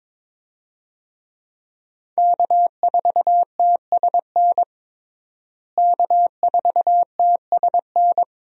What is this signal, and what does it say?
Morse code, heard as a keyed tone, about 22 wpm, 700 Hz. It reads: K4TSN K4TSN
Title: USA Call Signs - 5 Characters - Repeated 2x - 22wpm